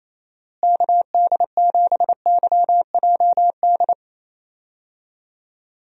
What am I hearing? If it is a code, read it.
Morse code, heard as a keyed tone, about 28 wpm, 700 Hz. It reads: KD7YJD
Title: USA Call Signs - 6 Characters - Repeated 2x - 28wpm